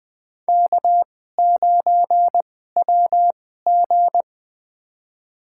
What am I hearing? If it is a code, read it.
Morse code, heard as a keyed tone, about 20 wpm, spaced as for 15 wpm, 700 Hz. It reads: K9WG
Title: USA Call Signs - 4 Characters - 2x Character Spacing - 20wpm